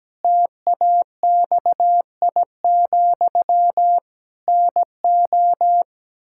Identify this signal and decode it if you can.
Morse code, heard as a keyed tone, about 17 wpm, 700 Hz. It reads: TAXI, NO
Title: Pangram Challenge v2 - 17wpm